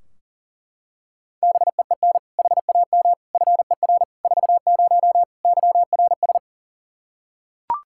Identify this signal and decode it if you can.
Morse code, heard as a keyed tone, about 40 wpm, 700 Hz. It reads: BEEN HAM FER 40 YRS
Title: Practice QSO - Exchange 3 - More Info - 40wpm